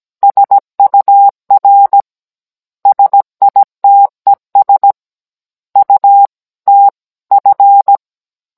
Morse code, heard as a keyed tone, about 17 wpm, 800 Hz. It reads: SUR SITES UTF
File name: Cours-NR18-16mm